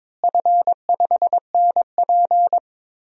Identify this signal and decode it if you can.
Morse code, heard as a keyed tone, about 22 wpm, 700 Hz. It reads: F5NP
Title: DXCC-100 Call Signs - 4 Characters - Repeated 3x - 22wpm